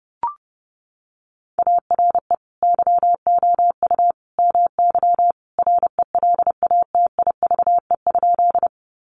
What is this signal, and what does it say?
Morse code, heard as a keyed tone, about 30 wpm, 700 Hz. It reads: ARE YOU MY RELATIVE?
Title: Sentences from Top 1400 Words 30wpm